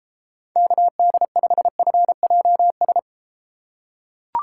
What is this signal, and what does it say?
Morse code, heard as a keyed tone, about 32 wpm, 700 Hz. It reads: KD5FJS